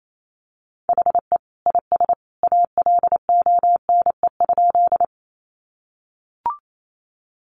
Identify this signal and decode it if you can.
Morse code, heard as a keyed tone, about 28 wpm, 700 Hz. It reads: HE IS ALONE?
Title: Sentences from Top 1000 Words 28wpm